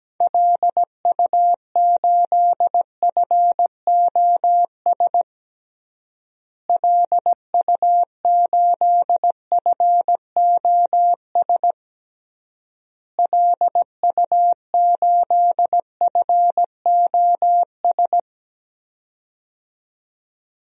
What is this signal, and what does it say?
Morse code, heard as a keyed tone, about 17 wpm, 700 Hz. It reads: LU8FOS LU8FOS LU8FOS
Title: DXCC-100 Call Signs - 6 Characters - Repeated 3x - 17wpm